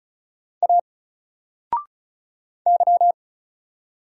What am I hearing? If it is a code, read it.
Morse code, heard as a keyed tone, about 34 wpm, 700 Hz. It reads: A Y
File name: Single-Letter-Number-35wpm